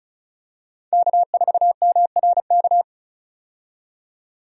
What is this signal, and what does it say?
Morse code, heard as a keyed tone, about 34 wpm, 700 Hz. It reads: K4MRK
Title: USA Call Signs - 5 Characters - 35wpm